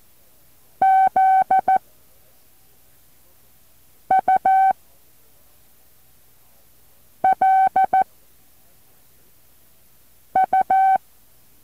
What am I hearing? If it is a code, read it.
Morse code, heard as a keyed tone, about 14 wpm, 760 Hz. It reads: Z U L U
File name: intl morse code2